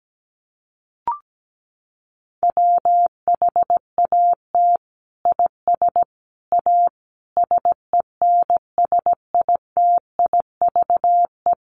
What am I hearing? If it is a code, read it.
Morse code, heard as a keyed tone, about 17 wpm, 700 Hz. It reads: WHAT IS A SENSITIVE
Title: Sentences from Top 1400 Words 17wpm